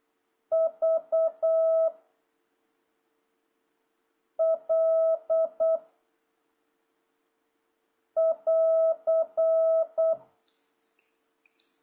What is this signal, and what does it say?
Morse code, heard as a keyed tone, about 8 wpm, 650 Hz. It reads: V L <AR>
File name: jh8cbh / 2026cw